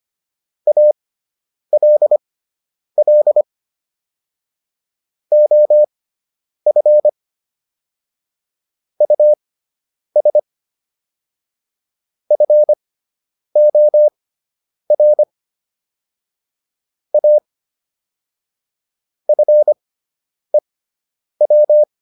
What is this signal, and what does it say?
Morse code, heard as a keyed tone, about 25 wpm, spaced as for 9 wpm, 600 Hz. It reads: ALL OF US FOR A FEW